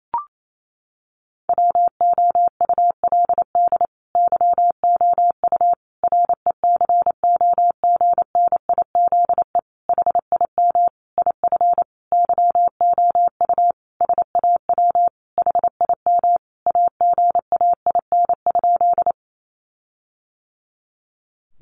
Morse code, heard as a keyed tone, about 28 wpm, 700 Hz. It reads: WOULD YOU RECOGNIZE HIM IF YOU SAW HIM AGAIN?
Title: Sentences from Top 1400 Words 28wpm